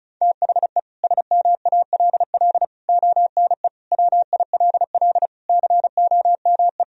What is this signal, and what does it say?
Morse code, heard as a keyed tone, about 36 wpm, 700 Hz. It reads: THE SMALL ONE WILL COME
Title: Sentences from Top 1000 Words - Review Part 1 - 35wpm